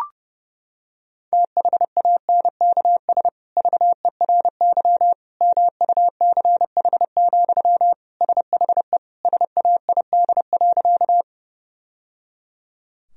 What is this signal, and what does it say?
Morse code, heard as a keyed tone, about 32 wpm, 700 Hz. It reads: THANKS VERY MUCH, SHE SAID.
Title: Sentences from Top 1000 Words - Review Part 3 - 30wpm